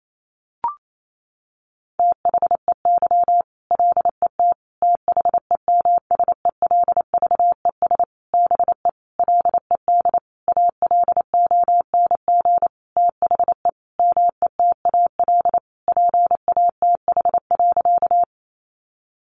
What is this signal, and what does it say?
Morse code, heard as a keyed tone, about 28 wpm, 700 Hz. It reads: THEY LET THEMSELVES BE LED ALONG THE METAL PATH.
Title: A Sound Of Thunder 28wpm